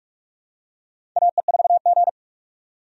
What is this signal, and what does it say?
Morse code, heard as a keyed tone, about 46 wpm, 700 Hz. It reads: AE4G
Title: USA Call Signs - 4 Characters - 45wpm